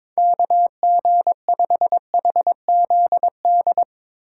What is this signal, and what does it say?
Morse code, heard as a keyed tone, about 22 wpm, 700 Hz. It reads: KG5HZD
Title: USA Call Signs - 6 Characters - 3x Character Spacing - 22wpm